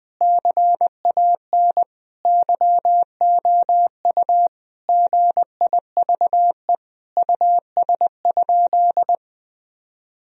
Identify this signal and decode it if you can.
Morse code, heard as a keyed tone, about 20 wpm, 700 Hz. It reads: CAN YOU GIVE US?
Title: Sentences from Top 1000 Words - Review Part 4 - 20wpm